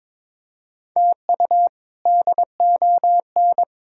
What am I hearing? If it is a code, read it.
Morse code, heard as a keyed tone, about 22 wpm, 700 Hz. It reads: TU DON
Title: K1USN SST QSO Practice - 22wpm